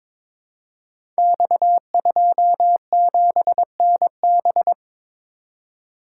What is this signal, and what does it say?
Morse code, heard as a keyed tone, about 22 wpm, 700 Hz. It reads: X27NB
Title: Code Groups - 5 Letter:Numbers - 22wpm